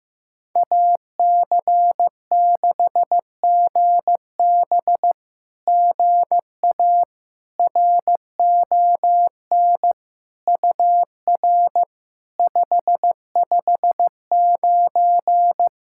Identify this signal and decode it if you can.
Morse code, heard as a keyed tone, about 15 wpm, 700 Hz. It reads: AC6GB GA RON UR 559
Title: SOTA QSO Practice 15wpm